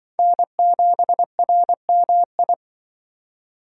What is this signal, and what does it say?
Morse code, heard as a keyed tone, about 24 wpm, 700 Hz. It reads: N7RMI
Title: Speed Racing - USA 1x3 Call Signs 20wpm